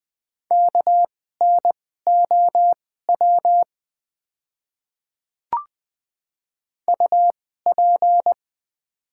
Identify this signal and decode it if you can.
Morse code, heard as a keyed tone, about 20 wpm, spaced as for 14 wpm, 690 Hz. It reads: KNOW UP
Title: 2x Character Spacing - Top 100 Words - 20wpm